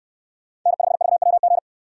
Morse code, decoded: N5VWG

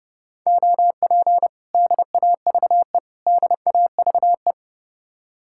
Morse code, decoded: OP DAVE DAVE

